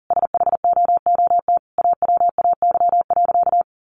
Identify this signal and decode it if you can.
Morse code, heard as a keyed tone, about 40 wpm, 700 Hz. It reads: SHOOT AWAY.
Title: Sentences from Top 1200 Words 40wpm